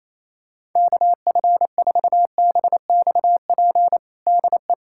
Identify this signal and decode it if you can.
Morse code, heard as a keyed tone, about 28 wpm, 700 Hz. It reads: KF4BXP DE
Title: USA Callsign DE Callsign - 6 Characters - 28wpm